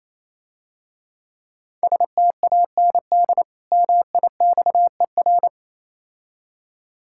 Morse code, heard as a keyed tone, about 27 wpm, 700 Hz. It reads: STAND MIXER
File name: Easy Words - Speed-Racing with Hints - 22wpm